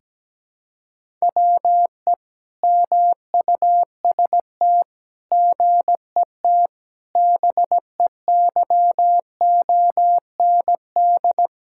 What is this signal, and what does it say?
Morse code, heard as a keyed tone, about 17 wpm, 700 Hz. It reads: WE MUST GET BEYOND
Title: Sentences from Top 1000 Words - Review Part 6 - 17wpm